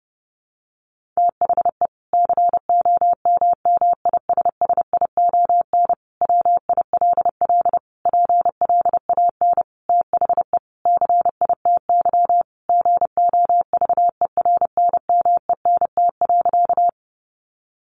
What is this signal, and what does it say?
Morse code, heard as a keyed tone, about 30 wpm, 700 Hz. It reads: THE COMMISSION WILL PLAN THE CITY GOVERNMENT.
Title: Sentences from Top 1400 Words 30wpm